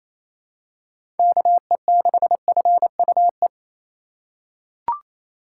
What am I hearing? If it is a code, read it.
Morse code, heard as a keyed tone, about 28 wpm, 700 Hz. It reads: KE6FUE
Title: USA Call Signs - 6 Characters - 8x Character Spacing - 28wpm